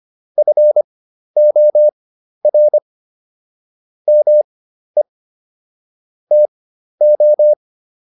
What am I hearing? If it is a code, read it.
Morse code, heard as a keyed tone, about 25 wpm, spaced as for 12 wpm, 600 Hz. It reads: FOR ME TO